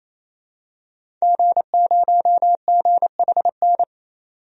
Morse code, heard as a keyed tone, about 28 wpm, 700 Hz. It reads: G0GHN
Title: DXCC-100 Call Signs - 5 Characters - 28wpm